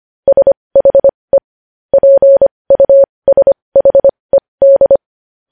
Morse code, heard as a keyed tone, about 25 wpm, spaced as for 20 wpm, 550 Hz. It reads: SHE PUSHED
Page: news-headlines-in-morse-code-at-20-wpm